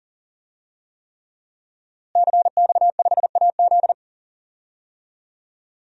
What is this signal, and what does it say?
Morse code, heard as a keyed tone, about 38 wpm, 700 Hz. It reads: CX5AZ
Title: DXCC-100 Call Signs - 5 Characters - Repeated 3x - 40wpm